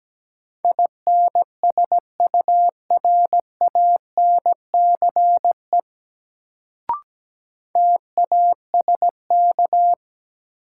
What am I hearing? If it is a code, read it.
Morse code, heard as a keyed tone, about 17 wpm, 700 Hz. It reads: INSURANCE TASK
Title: Sentences from Top 1100 Words - New Word Drill Alt - 17wpm